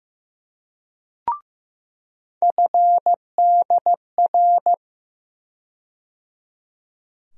Morse code, heard as a keyed tone, about 15 wpm, 700 Hz. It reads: FDR
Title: USA Call Sign Postfixes - 3 Letters 15wpm